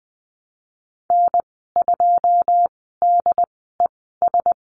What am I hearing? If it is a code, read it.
Morse code, heard as a keyed tone, about 19 wpm, spaced as for 14 wpm, 700 Hz. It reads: N2DES
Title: 088 Lesson 15g - Number 5 - Call Signs 20wpm